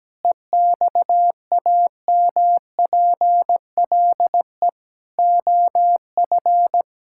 Morse code, heard as a keyed tone, about 17 wpm, 700 Hz. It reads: EXAMPLE OF
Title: Sentences from Top 1000 Words - Review Part 5 - 17wpm